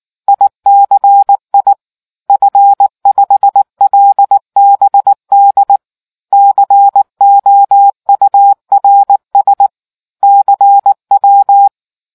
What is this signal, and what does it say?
Morse code, heard as a keyed tone, about 19 wpm, 800 Hz. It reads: ICI F5LBD COURS CW